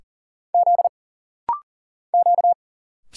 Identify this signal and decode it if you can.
Morse code, heard as a keyed tone, about 40 wpm, 700 Hz. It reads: Z Q
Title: Single Letter-Number Alt - Rapid-Fire 40wpm